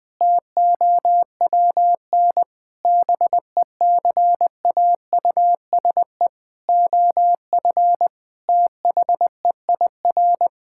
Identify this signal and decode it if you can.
Morse code, heard as a keyed tone, about 20 wpm, 700 Hz. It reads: TOWN BECAUSE OF THEIR
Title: Sentences from Top 1000 Words - Review Part 5 - 20wpm